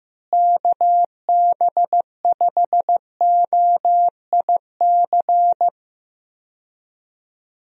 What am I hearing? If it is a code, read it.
Morse code, heard as a keyed tone, about 15 wpm, 700 Hz. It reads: KB5OIC
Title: USA Call Signs - 6 Characters - Repeated 2x - 15wpm